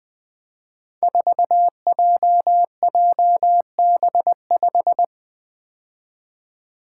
Morse code, heard as a keyed tone, about 20 wpm, 700 Hz. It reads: 4JJB5
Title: Code Groups - 5 Letter:Numbers - 20wpm